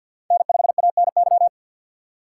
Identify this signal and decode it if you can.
Morse code, heard as a keyed tone, about 50 wpm, 700 Hz. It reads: N5ANY
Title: USA Call Signs - 5 Characters - 6x Character Spacing - 50wpm